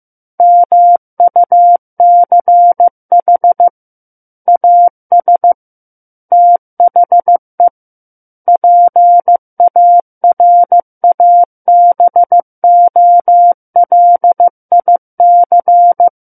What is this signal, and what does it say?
Morse code, heard as a keyed tone, about 15 wpm, 700 Hz. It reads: MUCH AS THE PARABOLIC
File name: Chapter 6 - 15 WPM